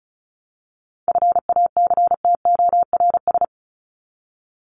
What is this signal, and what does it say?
Morse code, heard as a keyed tone, about 34 wpm, 700 Hz. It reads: FACTORS
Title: ES:S Suffix - Top 100 Words with Base Words 35wpm